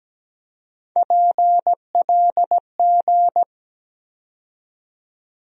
Morse code, heard as a keyed tone, about 17 wpm, 700 Hz. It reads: PLG
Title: USA Call Sign Postfixes - 3 Letters 17wpm